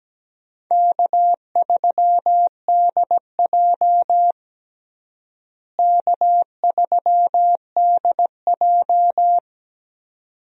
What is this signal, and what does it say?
Morse code, heard as a keyed tone, about 17 wpm, 700 Hz. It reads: K3DJ K3DJ
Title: USA Call Signs - 4 Characters - Repeated 2x - 17wpm